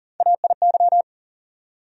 Morse code, heard as a keyed tone, about 37 wpm, 700 Hz. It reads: AIY
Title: USA Call Sign Postfixes - 3 Letters 40wpm